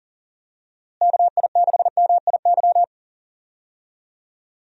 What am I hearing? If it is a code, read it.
Morse code, heard as a keyed tone, about 40 wpm, 700 Hz. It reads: KI6MIY
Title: USA Call Signs - 6 Characters - Repeated 2x - 40wpm